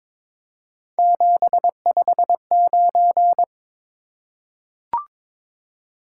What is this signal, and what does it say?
Morse code, heard as a keyed tone, about 22 wpm, 700 Hz. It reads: 759
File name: practice / 22wpm / Three Letter-Number Combinations - 22wpm